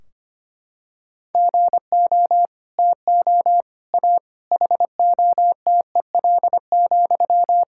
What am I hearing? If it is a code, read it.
Morse code, heard as a keyed tone, about 25 wpm, 700 Hz. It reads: GO TO A HOTEL,
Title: Sentences from Top 1600 Words 25wpm